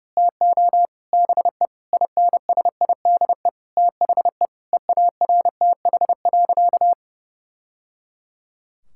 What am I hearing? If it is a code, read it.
Morse code, heard as a keyed tone, about 30 wpm, 700 Hz. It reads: TO BE INSIDE THE EARTH.
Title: Sentences from Top 1000 Words - Review Part 6 - 30wpm